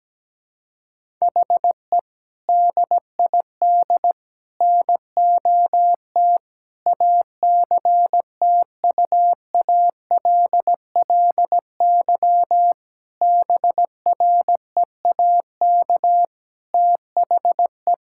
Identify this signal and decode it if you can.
Morse code, heard as a keyed tone, about 17 wpm, 700 Hz. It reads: HE DID NOT ACTUALLY BREAK THE
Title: Sentences from Top 1000 Words - Review Part 6 - 17wpm